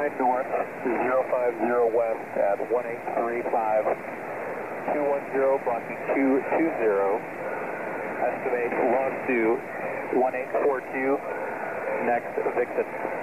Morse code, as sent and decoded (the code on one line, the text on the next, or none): none